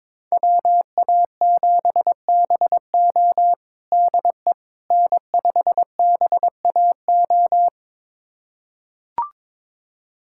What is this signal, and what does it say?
Morse code, heard as a keyed tone, about 21 wpm, 700 Hz. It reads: WA7BO DE N5BAO
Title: USA Callsign DE Callsign - 5 Characters - 22wpm